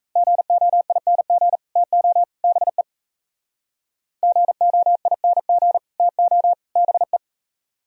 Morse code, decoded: GOING TO BE GOING TO BE

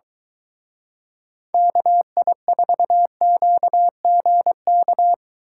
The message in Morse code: KI4QGK

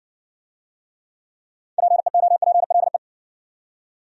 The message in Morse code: PEOPLE

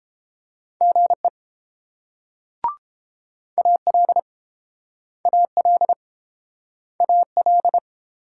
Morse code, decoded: GE AL AL AL

